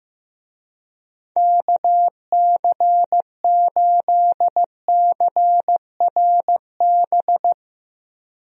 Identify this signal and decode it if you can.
Morse code, heard as a keyed tone, about 16 wpm, 700 Hz. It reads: KC8CRB
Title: USA Call Signs - 6 Characters - 15wpm